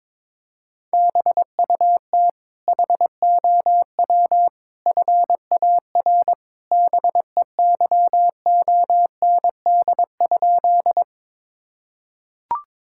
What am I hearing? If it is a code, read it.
Morse code, heard as a keyed tone, about 22 wpm, 700 Hz. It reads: BUT HOW FAR BEYOND?
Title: Sentences from Top 1000 Words - Review Part 6 - 22wpm